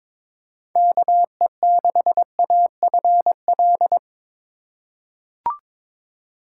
Morse code, heard as a keyed tone, about 22 wpm, 700 Hz. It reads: KE6AFL
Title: USA Call Signs - 6 Characters - 2x Character Spacing - 22wpm